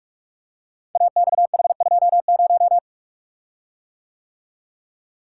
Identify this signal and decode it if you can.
Morse code, heard as a keyed tone, about 44 wpm, 700 Hz. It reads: AXH20
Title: Code Groups - 5 Letter:Numbers - 45wpm